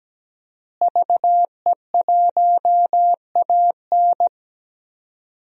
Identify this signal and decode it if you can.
Morse code, heard as a keyed tone, about 17 wpm, 700 Hz. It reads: VE1AN
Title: DXCC-100 Call Signs - 5 Characters - Repeated 2x - 17wpm